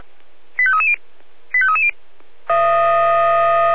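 Five-tone selective call (ZVEI), heard as quick runs of five tones, tones of about 70 ms. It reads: 85330 85330
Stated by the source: ZVEI five-tone selective call received with an SDR.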